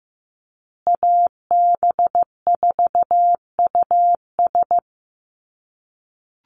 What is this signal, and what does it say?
Morse code, heard as a keyed tone, about 15 wpm, 700 Hz. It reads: AB4US
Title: USA Call Signs Repeated 2x 15wpm